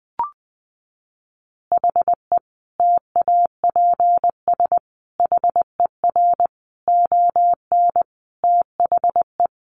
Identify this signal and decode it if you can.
Morse code, heard as a keyed tone, about 20 wpm, 700 Hz. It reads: HE TAPS HER ON THE
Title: Additional Easy Phrases 20wpm